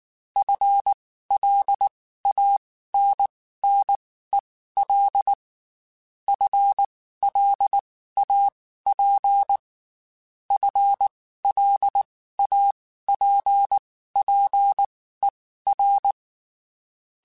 Morse code, heard as a keyed tone, about 19 wpm, spaced as for 14 wpm, 800 Hz. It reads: FLANNEL FLAP FLAPPER
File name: Spaced / FL Runs sp mp3s / FL Runs sp @18